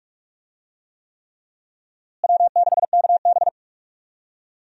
Morse code, heard as a keyed tone, about 46 wpm, 700 Hz. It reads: W6KB
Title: USA Call Signs - 4 Characters - Repeated 3x - 45wpm